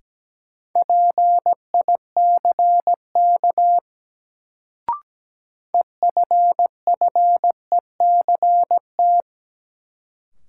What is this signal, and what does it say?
Morse code, heard as a keyed tone, about 17 wpm, 700 Hz. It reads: PICK EFFECT
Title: Sentences from Top 1000 Words - New Word Drill - 17wpm